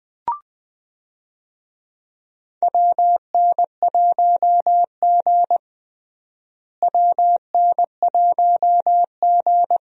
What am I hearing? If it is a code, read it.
Morse code, heard as a keyed tone, about 20 wpm, 700 Hz. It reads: WN1G WN1G
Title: USA Call Signs - 4 Characters - Repeated 3x - 20wpm